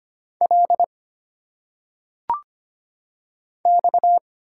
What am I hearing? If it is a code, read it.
Morse code, heard as a keyed tone, about 25 wpm, 700 Hz. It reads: L X
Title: Single Letters - Phonetic Pronunciation 25wpm